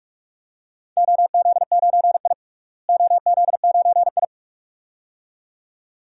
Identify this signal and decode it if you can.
Morse code, heard as a keyed tone, about 47 wpm, 700 Hz. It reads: OZ9I OZ9I
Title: DXCC-100 Call Signs - 4 Characters - Repeated 2x - 45wpm